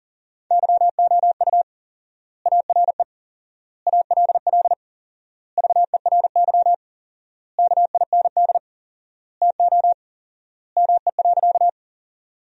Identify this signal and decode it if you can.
Morse code, heard as a keyed tone, about 40 wpm, 700 Hz. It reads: YOU ARE ALL VERY KIND TO ME.
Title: 4x Word-Spacing - Sentences from Top 200 Words - 40wpm